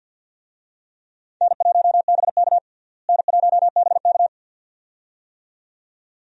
Morse code, heard as a keyed tone, about 49 wpm, 700 Hz. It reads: N1BK N1BK
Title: USA Call Signs - 4 Characters - Repeated 2x - 50wpm